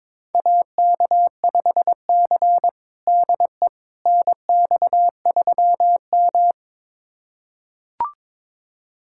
Morse code, heard as a keyed tone, about 22 wpm, 700 Hz. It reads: AK5C DE NX3M
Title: USA Callsign DE Callsign - 4 Characters - 22wpm